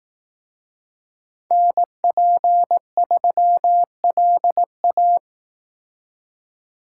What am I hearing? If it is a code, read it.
Morse code, heard as a keyed tone, about 18 wpm, 700 Hz. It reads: NP3LA
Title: Speed Racing - USA Call Signs 15wpm